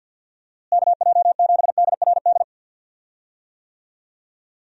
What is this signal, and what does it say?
Morse code, heard as a keyed tone, about 53 wpm, 700 Hz. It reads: KJ7DRD
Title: USA Call Signs - 6 Characters - Repeated 2x - 50wpm